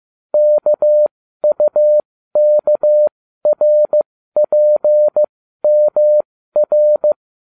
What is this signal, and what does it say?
Morse code, heard as a keyed tone, about 15 wpm, spaced as for 12 wpm, 600 Hz. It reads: KUKRPMR